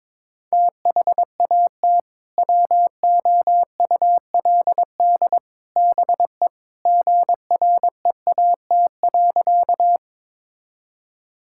Morse code, heard as a keyed tone, about 21 wpm, 700 Hz. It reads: THAT WOULD BE GREAT.
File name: practice / 22wpm / Sentences with 3-Word Phrases - 22wpm